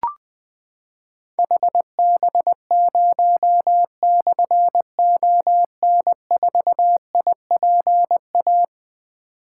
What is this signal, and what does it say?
Morse code, heard as a keyed tone, about 20 wpm, 700 Hz. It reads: HB0/ON4IPA
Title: Difficult Call Signs 20wpm